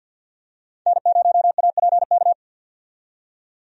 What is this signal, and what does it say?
Morse code, heard as a keyed tone, about 50 wpm, 700 Hz. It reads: N0APK
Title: USA Call Signs - 5 Characters - 50wpm